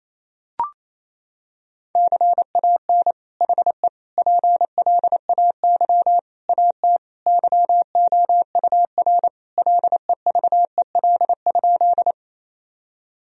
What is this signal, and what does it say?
Morse code, heard as a keyed tone, about 28 wpm, 700 Hz. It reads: CAN HE PLAY AT YOUR LEVEL?
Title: Sentences from Top 1000 Words - Review Part 4 - 28wpm